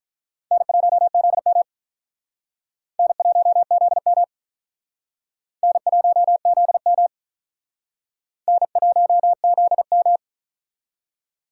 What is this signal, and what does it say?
Morse code, heard as a keyed tone, about 45 wpm, 700 Hz. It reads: N1ZM N1ZM N1ZM N1ZM